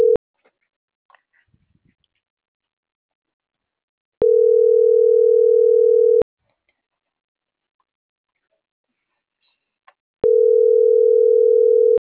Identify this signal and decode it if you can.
Ringback tone (North America), 440 + 480 Hz.